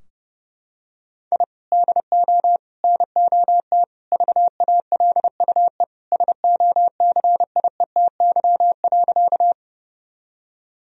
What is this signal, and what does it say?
Morse code, heard as a keyed tone, about 32 wpm, 700 Hz. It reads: I DO NOT VALUE SOCIETY.
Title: Sentences from Top 1000 Words - Review Part 4 - 30wpm